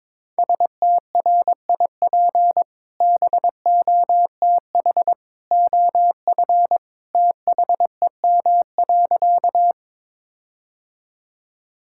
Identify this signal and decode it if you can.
Morse code, heard as a keyed tone, about 22 wpm, 700 Hz. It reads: STRIP BOTH OF THEM.
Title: Sentences from Top 1600 Words 22wpm